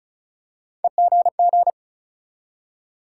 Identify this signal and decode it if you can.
Morse code, heard as a keyed tone, about 36 wpm, 700 Hz. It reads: EGG
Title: Three Letter Words - 35wpm